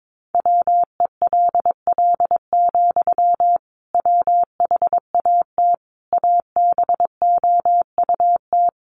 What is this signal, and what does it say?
Morse code, heard as a keyed tone, about 22 wpm, 700 Hz. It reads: WELL, WHAT ABOUT
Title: Sentences from Top 500 Words 22wpm